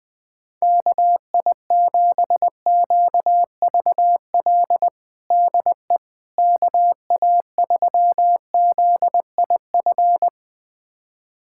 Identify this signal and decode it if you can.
Morse code, heard as a keyed tone, about 20 wpm, 700 Hz. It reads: KI7QVL DE KA3ZIF
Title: USA Callsign DE Callsign 20wpm